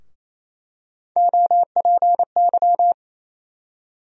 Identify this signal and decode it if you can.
Morse code, heard as a keyed tone, about 28 wpm, 700 Hz. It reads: OPY
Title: USA Call Sign Postfixes - 3 Letters 28wpm